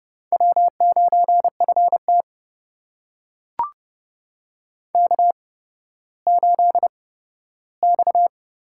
Morse code, heard as a keyed tone, about 30 wpm, spaced as for 15 wpm, 700 Hz. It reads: W9FT K8X
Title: USA Call Signs - 4 Characters - 8x Character Spacing - 30wpm